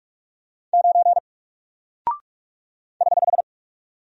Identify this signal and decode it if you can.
Morse code, heard as a keyed tone, about 44 wpm, 700 Hz. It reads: 9 <HH>
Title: Single Character (Letter:Number:Punctuation:Prosign) Alt 45wpm